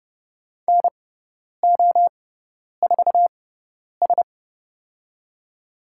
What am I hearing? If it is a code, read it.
Morse code, heard as a keyed tone, about 31 wpm, spaced as for 10 wpm, 700 Hz. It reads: NO4S
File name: Lesson 17g - Letter u - Call Signs 30wpm